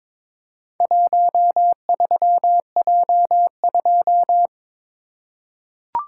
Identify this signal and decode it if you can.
Morse code, heard as a keyed tone, about 23 wpm, 690 Hz. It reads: 13J2
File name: practice / 22wpm / Four Letter-Number Combinations - 22wpm